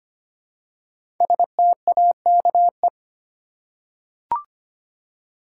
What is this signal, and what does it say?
Morse code, heard as a keyed tone, about 25 wpm, 700 Hz. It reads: STAKE
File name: 5 Letter Words - First 3 Letters Given - 25wpm